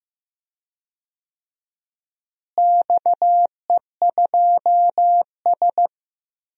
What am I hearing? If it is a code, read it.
Morse code, heard as a keyed tone, about 15 wpm, 700 Hz. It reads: XE2S
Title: DXCC-100 Call Signs - 4 Characters - Repeated 3x - 15wpm